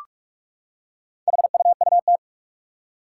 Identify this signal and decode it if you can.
Morse code, heard as a keyed tone, about 44 wpm, 700 Hz. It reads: HUUT
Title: Four Challenging Letter-Number Combinations 45wpm